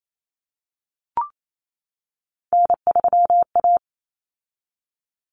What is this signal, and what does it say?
Morse code, heard as a keyed tone, about 28 wpm, 700 Hz. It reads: N3A
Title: Common Cut Numbers - 3-Digit 28wpm